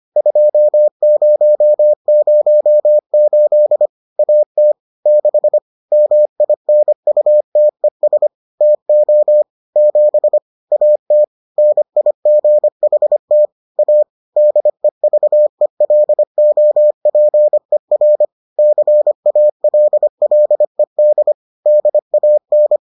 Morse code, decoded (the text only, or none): 2008 AT 6 MINUTES TO 7 AT NIGHT A DEVELOPER CALLED DAN